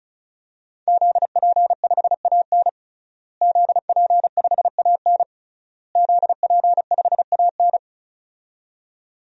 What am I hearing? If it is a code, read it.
Morse code, heard as a keyed tone, about 36 wpm, 700 Hz. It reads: ZP5AN ZP5AN ZP5AN
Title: DXCC-100 Call Signs - 5 Characters - Repeated 3x - 35wpm